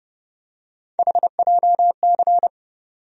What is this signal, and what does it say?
Morse code, heard as a keyed tone, about 30 wpm, 700 Hz. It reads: HJC
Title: USA Call Sign Postfixes - 3 Letters 30wpm